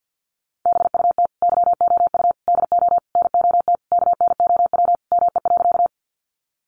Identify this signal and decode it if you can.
Morse code, heard as a keyed tone, about 50 wpm, 700 Hz. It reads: BUT YOU DO NOT KNOW ME.
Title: Sentences from Top 600 Words 50wpm